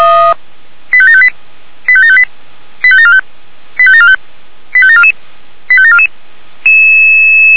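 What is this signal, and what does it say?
Five-tone selective call (ZVEI), heard as quick runs of five tones, tones of about 70 ms. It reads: 85758 85758 85745 85745 85744 85744
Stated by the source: ZVEI five-tone selective call, code 85730, received on an SDR.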